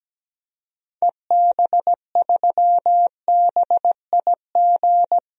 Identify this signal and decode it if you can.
Morse code, heard as a keyed tone, about 17 wpm, 700 Hz. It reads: EB3BIG
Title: DXCC-100 Call Signs - 6 Characters - 17wpm